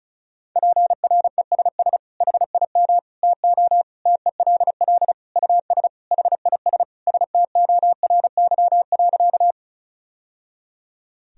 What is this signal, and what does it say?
Morse code, heard as a keyed tone, about 34 wpm, 700 Hz. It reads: PRESS HIM TO TELL US HIS STORY.
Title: Sentences from Top 1000 Words - Review Part 1 - 35wpm